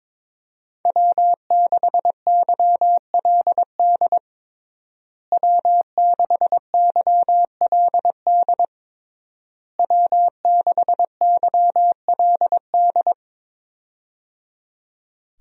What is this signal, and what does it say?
Morse code, heard as a keyed tone, about 22 wpm, 700 Hz. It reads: W6YLD W6YLD W6YLD
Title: USA Call Signs - 5 Characters - Repeated 3x - 22wpm